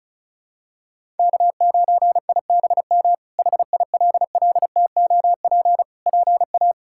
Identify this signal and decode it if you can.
Morse code, heard as a keyed tone, about 34 wpm, 700 Hz. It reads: K9IBM HILLTOP PA
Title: K1USN SST QSO Practice - 35wpm